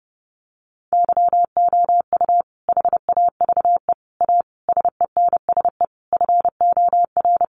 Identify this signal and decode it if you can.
Morse code, heard as a keyed tone, about 30 wpm, 700 Hz. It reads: YOU HAVE A SENSE FOR
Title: Sentences from Top 700 Words 30wpm